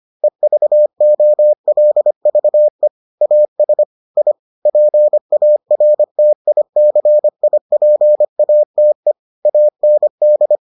Morse code, heard as a keyed tone, about 25 wpm, 600 Hz. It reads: EVOLVE AS I PARTICIPATE AND